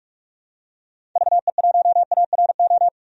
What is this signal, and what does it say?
Morse code, heard as a keyed tone, about 46 wpm, 700 Hz. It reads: VE1ARO
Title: DXCC-100 Call Signs - 6 Characters - 45wpm